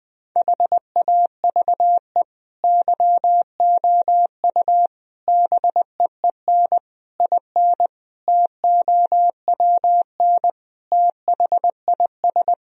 Morse code, heard as a keyed tone, about 20 wpm, 700 Hz. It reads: HAVE YOU BEEN IN TOWN THIS
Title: Sentences from Top 1000 Words - Review Part 2 - 20wpm